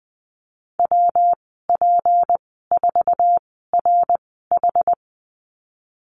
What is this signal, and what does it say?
Morse code, heard as a keyed tone, about 20 wpm, spaced as for 15 wpm, 700 Hz. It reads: WP4RH
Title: 186 Lesson 30g - Letter b - Call Signs 20wpm